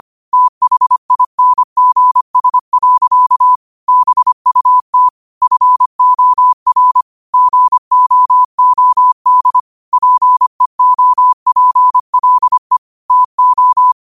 Morse code, heard as a keyed tone, about 25 wpm, 1000 Hz. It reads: THINGS. BUT FOR GOOD PEOPLE TO
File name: qotdfiles / 25wpm